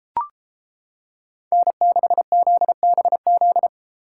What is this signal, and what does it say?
Morse code, heard as a keyed tone, about 33 wpm, 700 Hz. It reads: N6ZBZ